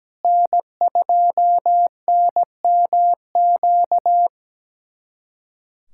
Morse code, heard as a keyed tone, about 17 wpm, 700 Hz. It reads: N2NMQ
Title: USA Callsign DE Callsign - 5 Characters - 17wpm